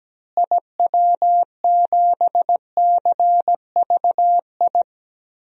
Morse code, heard as a keyed tone, about 17 wpm, 700 Hz. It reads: IW7CVI
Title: DXCC-100 Call Signs - 6 Characters - Repeated 3x - 17wpm